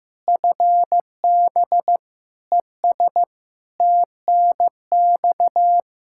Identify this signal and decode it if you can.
Morse code, heard as a keyed tone, about 15 wpm, 700 Hz. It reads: FB ES TNX
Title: Practice QSO - Exchange 2 - Info - 15wpm